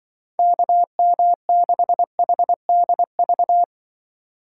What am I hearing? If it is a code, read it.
Morse code, heard as a keyed tone, about 24 wpm, 700 Hz. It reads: KM6HDV
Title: Speed Racing - USA 2x3 Call Signs 20wpm